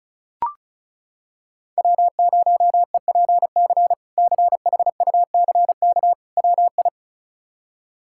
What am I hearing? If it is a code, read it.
Morse code, heard as a keyed tone, about 36 wpm, 700 Hz. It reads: W0EPC CHUCK WI
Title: K1USN SST QSO Practice - 35wpm